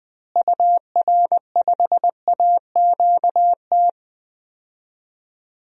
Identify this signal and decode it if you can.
Morse code, heard as a keyed tone, about 20 wpm, 700 Hz. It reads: UR5AQT
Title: DXCC-100 Call Signs - 6 Characters - Repeated 2x - 20wpm